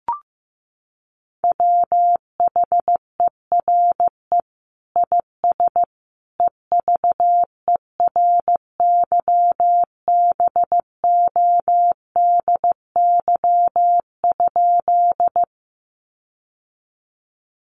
Morse code, heard as a keyed tone, about 15 wpm, 700 Hz. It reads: WHERE IS EVERYBODY?
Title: Sentences from Top 1400 Words 15wpm